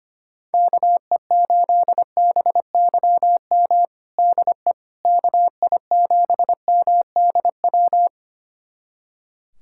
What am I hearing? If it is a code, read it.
Morse code, heard as a keyed tone, about 25 wpm, 700 Hz. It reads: KE8BYM DE KI7MDW
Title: USA Callsign DE Callsign - 6 Characters - 25wpm